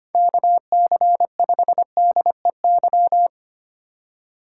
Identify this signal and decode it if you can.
Morse code, heard as a keyed tone, about 25 wpm, 700 Hz. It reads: KC5DEY
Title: USA Call Signs - 6 Characters - 4x Character Spacing - 25wpm